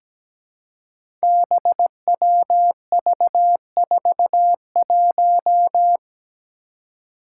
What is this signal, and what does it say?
Morse code, heard as a keyed tone, about 17 wpm, 700 Hz. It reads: BWV41
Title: Five Challenging Letter-Number Combinations 17wpm